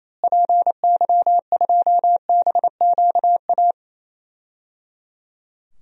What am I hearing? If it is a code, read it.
Morse code, heard as a keyed tone, about 28 wpm, 700 Hz. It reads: PY2BQA
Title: DXCC-100 Call Signs - 6 Characters - Repeated 3x - 28wpm